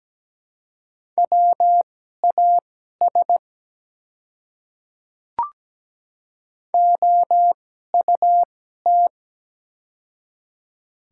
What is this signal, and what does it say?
Morse code, heard as a keyed tone, about 17 wpm, spaced as for 12 wpm, 690 Hz. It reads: WAS OUT
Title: 2x Character Spacing - Top 100 Words - 17wpm